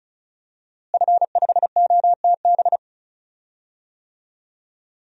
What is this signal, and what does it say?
Morse code, heard as a keyed tone, about 34 wpm, 700 Hz. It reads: F5OTB
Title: DXCC-100 Call Signs - 5 Characters - Repeated 3x - 35wpm